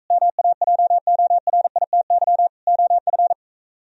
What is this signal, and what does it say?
Morse code, heard as a keyed tone, about 44 wpm, 700 Hz. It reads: MAJORITY OF